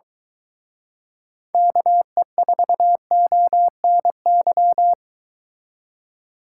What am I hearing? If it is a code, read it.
Morse code, heard as a keyed tone, about 22 wpm, 700 Hz. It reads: KE4ONY